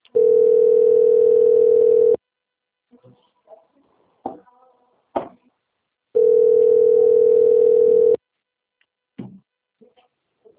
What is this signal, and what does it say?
Ringback tone (North America), 440 + 480 Hz.
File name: out- - - - - .47617